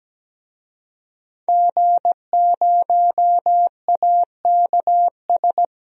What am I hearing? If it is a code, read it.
Morse code, heard as a keyed tone, about 17 wpm, 700 Hz. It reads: G0AKS
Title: DXCC-100 Call Signs - 5 Characters - Repeated 2x - 17wpm